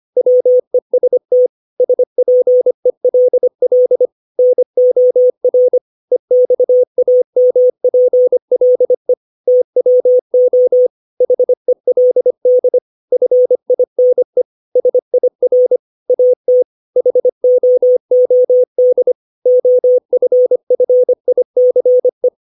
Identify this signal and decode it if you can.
Morse code, heard as a keyed tone, about 25 wpm, 500 Hz. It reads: WEST SPELL NOR EXAMPLE TWO HELD FINE SIR AT HOOD OFFICE
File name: 632 words / 632 Words 25WPM